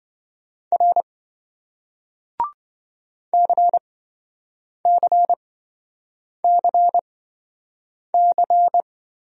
Morse code, heard as a keyed tone, about 26 wpm, spaced as for 7 wpm, 700 Hz. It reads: R CCCC